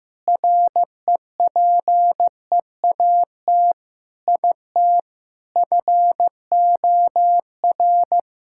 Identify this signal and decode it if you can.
Morse code, heard as a keyed tone, about 15 wpm, 700 Hz. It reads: REPEAT IT FOR
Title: Sentences from Top 1600 Words 15wpm